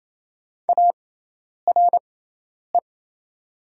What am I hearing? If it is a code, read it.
Morse code, heard as a keyed tone, about 28 wpm, spaced as for 10 wpm, 700 Hz. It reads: ARE
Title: 6x Character Spacing - Top 300 Words - 28wpm